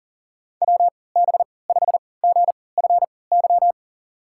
Morse code, heard as a keyed tone, about 40 wpm, spaced as for 23 wpm, 700 Hz. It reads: WB5GFY
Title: USA Call Signs - 6 Characters - 3x Character Spacing - 40wpm